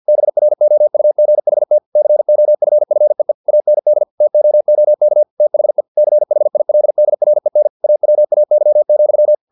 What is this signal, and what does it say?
Morse code, heard as a keyed tone, about 50 wpm, 600 Hz. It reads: BROUGHT COFFEE AND TOOK THE CHILDREN AWAY,
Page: Never 49wpm